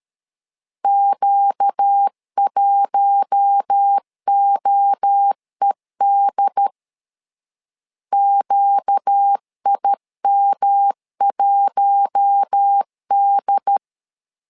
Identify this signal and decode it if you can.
Morse code, heard as a keyed tone, about 13 wpm, 800 Hz. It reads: Q1OED QIM1D
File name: LEZIONI / 4 Q1 12wpm 2028